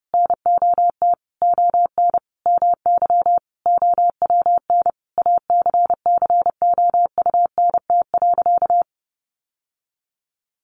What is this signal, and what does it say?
Morse code, heard as a keyed tone, about 28 wpm, 700 Hz. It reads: NOT ON MY OWN ACCOUNT.
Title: Sentences from Top 1400 Words 30wpm